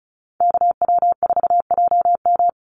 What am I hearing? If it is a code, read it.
Morse code, heard as a keyed tone, about 34 wpm, 700 Hz. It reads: KW4JM
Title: NA SOTA Activator Callsigns 35wpm